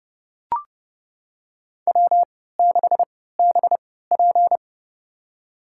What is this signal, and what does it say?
Morse code, heard as a keyed tone, about 30 wpm, spaced as for 17 wpm, 700 Hz. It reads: W6BP
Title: USA Call Signs - 4 Characters - 3x Character Spacing - 30wpm